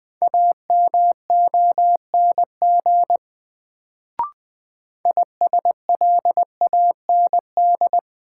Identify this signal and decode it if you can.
Morse code, heard as a keyed tone, about 20 wpm, 700 Hz. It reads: AMONG ISLAND
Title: Sentences from Top 500 Words - New Word Drill - 20wpm